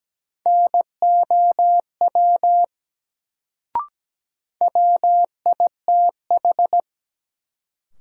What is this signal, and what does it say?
Morse code, heard as a keyed tone, about 17 wpm, 700 Hz. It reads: NOW WITH
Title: Sentences from Top 100 Words - New Word Drill - 17wpm